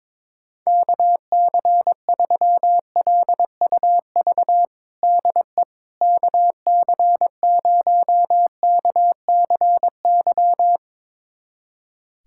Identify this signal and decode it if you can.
Morse code, heard as a keyed tone, about 22 wpm, 700 Hz. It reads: KC3LUV DE KC0KCY
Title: USA Callsign DE Callsign 22wpm